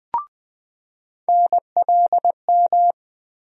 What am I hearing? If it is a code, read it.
Morse code, heard as a keyed tone, about 20 wpm, 700 Hz. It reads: NLM
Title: USA Call Sign Postfixes - 3 Letters 20wpm